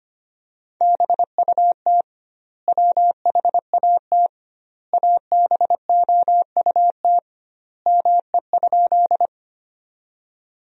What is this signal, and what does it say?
Morse code, heard as a keyed tone, about 25 wpm, 700 Hz. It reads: BUT WHAT ABOUT ME?
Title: 2x Word-Spacing - Sentences from Top 200 Words - 25wpm